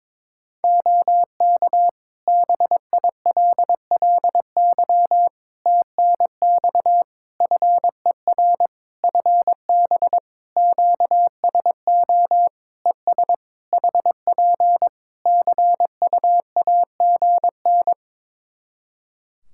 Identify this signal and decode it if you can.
Morse code, heard as a keyed tone, about 22 wpm, 700 Hz. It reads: OK BILLY TNX FER FB QSO ES HP CUAGN
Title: Practice QSO - Exchange 4 - Ending - 22wpm